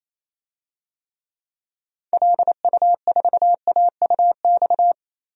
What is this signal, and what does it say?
Morse code, heard as a keyed tone, about 28 wpm, 700 Hz. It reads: LU4AUX
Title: DXCC-100 Call Signs - 6 Characters - Repeated 2x - 28wpm